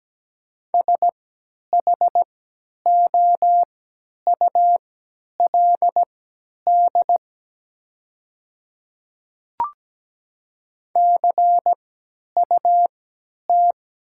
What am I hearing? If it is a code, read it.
Morse code, heard as a keyed tone, about 17 wpm, spaced as for 10 wpm, 700 Hz. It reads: SHOULD CUT
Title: 3x Character Spacing - Top 500 Words - 17wpm